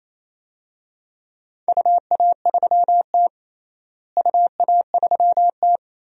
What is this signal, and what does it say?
Morse code, heard as a keyed tone, about 28 wpm, 700 Hz. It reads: UA3T UA3T
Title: DXCC-100 Call Signs - 4 Characters - Repeated 3x - 28wpm